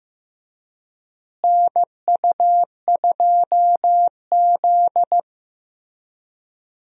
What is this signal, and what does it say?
Morse code, heard as a keyed tone, about 15 wpm, 700 Hz. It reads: NU2Z
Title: USA Call Signs - 4 Characters - Repeated 3x - 15wpm